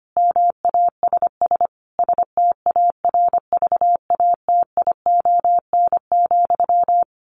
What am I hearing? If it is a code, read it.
Morse code, heard as a keyed tone, about 25 wpm, 700 Hz. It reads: MASS STARVATION,
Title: A Sound Of Thunder 25wpm